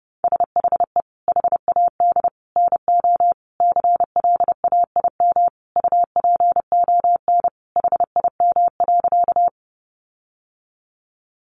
Morse code, heard as a keyed tone, about 30 wpm, 700 Hz. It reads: SHE HAD NO CLAIM UPON HIM.
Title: Sentences from Top 1500 Words 30wpm